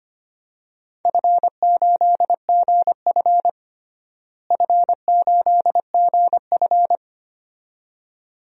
Morse code, heard as a keyed tone, about 25 wpm, 700 Hz. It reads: F8GF F8GF
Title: DXCC-100 Call Signs - 4 Characters - Repeated 2x - 25wpm